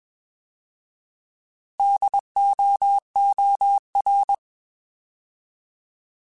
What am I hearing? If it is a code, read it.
Morse code, heard as a keyed tone, about 21 wpm, 800 Hz. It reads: DOOR
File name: Alphabetical / Alphabetical mp3s / Ds @20